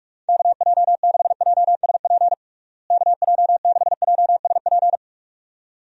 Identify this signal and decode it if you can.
Morse code, heard as a keyed tone, about 46 wpm, 700 Hz. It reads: KJ6JSP KJ6JSP
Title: USA Call Signs - 6 Characters - Repeated 2x - 45wpm